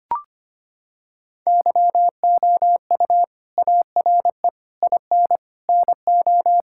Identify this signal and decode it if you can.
Morse code, heard as a keyed tone, about 25 wpm, 700 Hz. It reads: YOU ARE IN NO
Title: Sentences from Top 1600 Words 25wpm